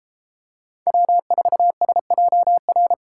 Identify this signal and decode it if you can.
Morse code, heard as a keyed tone, about 29 wpm, 700 Hz. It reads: W4SJR